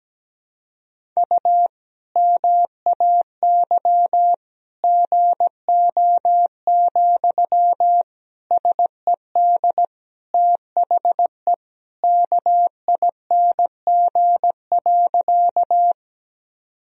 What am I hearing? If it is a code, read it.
Morse code, heard as a keyed tone, about 17 wpm, 700 Hz. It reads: U MAY GO, SED THE KING.
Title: Sentences from Top 400 Words and Abbreviations 17wpm